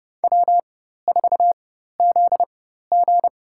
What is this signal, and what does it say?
Morse code, heard as a keyed tone, about 30 wpm, spaced as for 14 wpm, 700 Hz. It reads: W4ZG
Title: USA Call Signs - 4 Characters - 4x Character Spacing - 30wpm